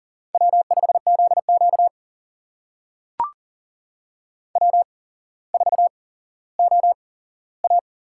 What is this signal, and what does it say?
Morse code, heard as a keyed tone, about 40 wpm, spaced as for 17 wpm, 700 Hz. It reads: W5ZQ W4OA
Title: USA Call Signs - 4 Characters - 8x Character Spacing - 40wpm